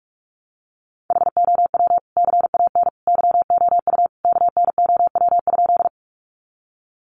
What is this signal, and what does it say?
Morse code, heard as a keyed tone, about 44 wpm, 700 Hz. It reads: HOW CAN YOU KNOW?
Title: Sentences from Top 100 Words 45wpm